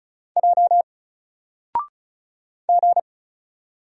Morse code, heard as a keyed tone, about 36 wpm, 700 Hz. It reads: J G
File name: Single-Letter-Number-35wpm